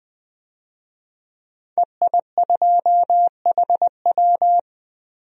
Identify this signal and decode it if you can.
Morse code, heard as a keyed tone, about 20 wpm, 700 Hz. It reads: EI2HW
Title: DXCC-100 Call Signs - 5 Characters - Repeated 3x - 20wpm